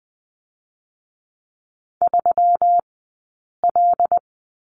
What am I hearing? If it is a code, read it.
Morse code, heard as a keyed tone, about 20 wpm, 700 Hz. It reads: 3 L
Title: 120 Lesson 20d - Number 3 - Two Characters 20wpm